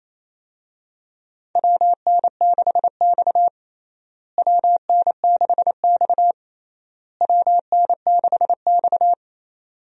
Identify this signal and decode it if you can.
Morse code, heard as a keyed tone, about 28 wpm, 700 Hz. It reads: WN6X WN6X WN6X
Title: USA Call Signs - 4 Characters - Repeated 3x - 28wpm